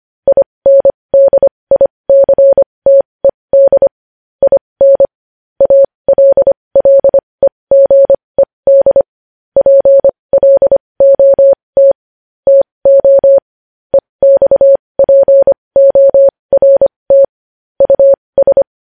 Morse code, heard as a keyed tone, about 25 wpm, spaced as for 20 wpm, 550 Hz. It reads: INDICTED IN ALLEGED PLOT TO EXPORT US
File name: podcast / News headlines-1x-25x20